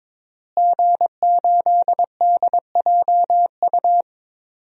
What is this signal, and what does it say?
Morse code, heard as a keyed tone, about 22 wpm, 700 Hz. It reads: G8DJU
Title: DXCC-100 Call Signs - 5 Characters - Repeated 3x - 22wpm